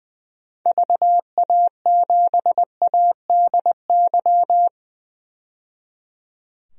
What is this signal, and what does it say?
Morse code, heard as a keyed tone, about 20 wpm, 700 Hz. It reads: VA7ADY
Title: DXCC-100 Call Signs - 6 Characters - 20wpm